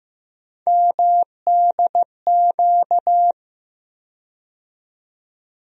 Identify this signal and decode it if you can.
Morse code, heard as a keyed tone, about 15 wpm, 700 Hz. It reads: MDQ
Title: USA Call Sign Postfixes - 3 Letters 15wpm